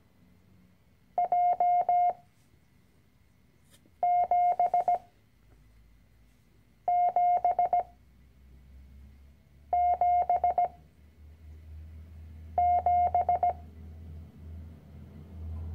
Morse code, heard as a keyed tone, about 17 wpm, 710 Hz. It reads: J 7 7 7 7
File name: CW-Online-Kurs / Anfänger / Lektion 010